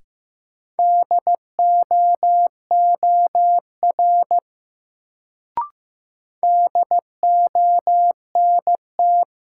Morse code, heard as a keyed tone, about 15 wpm, 700 Hz. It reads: DOOR DONT
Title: Sentences from Top 300 Words - New Word Drill - 15wpm